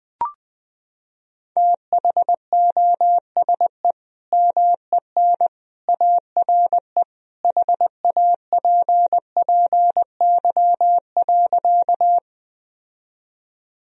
Morse code, heard as a keyed tone, about 21 wpm, 700 Hz. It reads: THOSE MEN ARE HAPPY.
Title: Sentences from Top 1000 Words - Review Part 1 - 20wpm